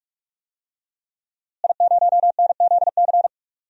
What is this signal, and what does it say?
Morse code, heard as a keyed tone, about 42 wpm, 700 Hz. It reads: I0NZC
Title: DXCC-100 Call Signs - 5 Characters - Repeated 3x - 45wpm